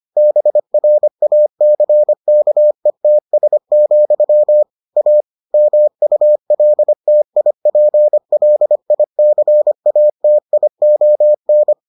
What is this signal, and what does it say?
Morse code, heard as a keyed tone, about 25 wpm, 600 Hz. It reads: BRACKETS, A MULTIPLICATION